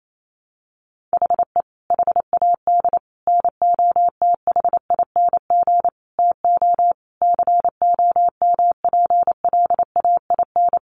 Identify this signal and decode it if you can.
Morse code, heard as a keyed tone, about 28 wpm, 700 Hz. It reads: HE HAD NOTHING TO COMPLAIN
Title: Sentences from Top 1300 Words 28wpm